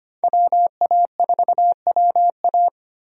Morse code, heard as a keyed tone, about 25 wpm, 700 Hz. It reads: WA4WA
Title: USA Call Signs - 5 Characters - 25wpm